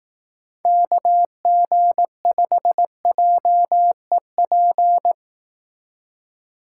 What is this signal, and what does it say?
Morse code, heard as a keyed tone, about 18 wpm, 700 Hz. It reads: KG5JEP
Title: Speed Racing - USA Call Signs 15wpm